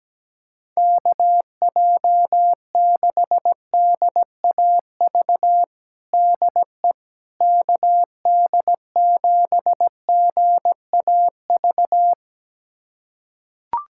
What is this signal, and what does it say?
Morse code, heard as a keyed tone, about 17 wpm, 700 Hz. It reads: KJ6DAV DE KD7GAV
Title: USA Callsign DE Callsign - 6 Characters - 17wpm